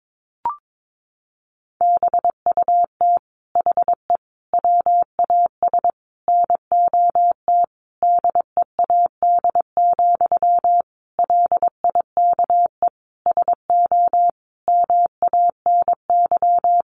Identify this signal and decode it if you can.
Morse code, heard as a keyed tone, about 22 wpm, 700 Hz. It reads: BUT HE WAS NOT DEAD, LIKE SO MANY
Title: Sentences from Top 1500 Words 22wpm